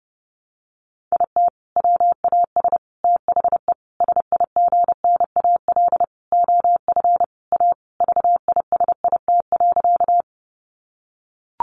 Morse code, heard as a keyed tone, about 30 wpm, 700 Hz. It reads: IT WAS THE SIGNAL OF A VISIT.
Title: Sentences from Top 1400 Words 30wpm